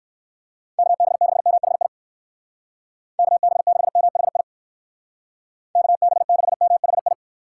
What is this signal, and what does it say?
Morse code, heard as a keyed tone, about 54 wpm, 700 Hz. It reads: KB6MHI KB6MHI KB6MHI
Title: Speed Racing - USA 2x3 Call Signs 45wpm